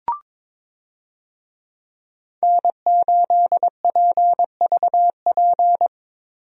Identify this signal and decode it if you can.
Morse code, heard as a keyed tone, about 22 wpm, 700 Hz. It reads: N8PVP
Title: USA Call Signs - 5 Characters - Repeated 3x - 22wpm